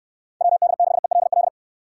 Morse code, heard as a keyed tone, about 62 wpm, 700 Hz. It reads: WD6ELL